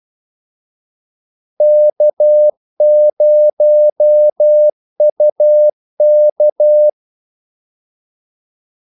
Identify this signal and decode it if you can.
Morse code, heard as a keyed tone, about 12 wpm, 600 Hz. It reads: K0UK
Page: 2 ekstra ordmellemrum mellem kaldesignalerne